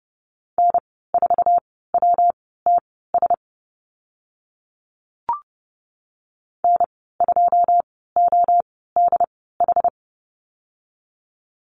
Morse code, heard as a keyed tone, about 30 wpm, spaced as for 17 wpm, 700 Hz. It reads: N4WTS N2ODH
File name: Lesson 19g - Letter w - Call Signs 30wpm